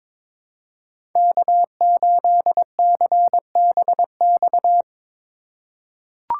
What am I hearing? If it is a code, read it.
Morse code, heard as a keyed tone, about 22 wpm, 700 Hz. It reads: K8CBX
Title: USA Call Signs - 8x Character Spacing - 22wpm